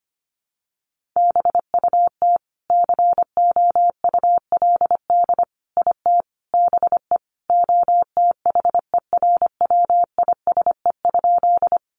Morse code, heard as a keyed tone, about 25 wpm, 700 Hz. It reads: BUT COULD IT BE OTHERWISE?
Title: Sentences from Top 1200 Words 25wpm